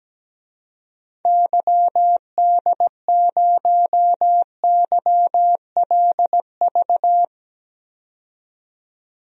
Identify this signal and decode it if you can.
Morse code, heard as a keyed tone, about 17 wpm, 700 Hz. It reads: YD0YLV
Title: DXCC-100 Call Signs - 6 Characters - 17wpm